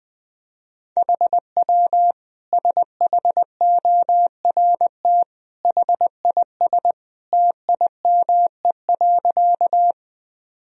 Morse code, heard as a keyed tone, about 20 wpm, 700 Hz. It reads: HW SHORT HIS TIME.
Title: Sentences from Top 400 Words and Abbreviations 20wpm